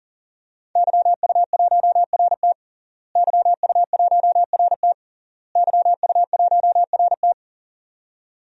Morse code, heard as a keyed tone, about 40 wpm, 700 Hz. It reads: YU1RT YU1RT YU1RT
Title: DXCC-100 Call Signs - 5 Characters - Repeated 3x - 40wpm